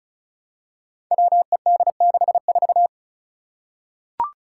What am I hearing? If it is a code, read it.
Morse code, heard as a keyed tone, about 36 wpm, 700 Hz. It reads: WED64
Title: Five Challenging Letter-Number Combinations 35wpm